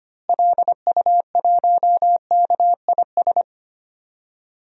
Morse code, heard as a keyed tone, about 25 wpm, 700 Hz. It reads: LU1KIS
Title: DXCC-100 Call Signs - 6 Characters - Repeated 2x - 25wpm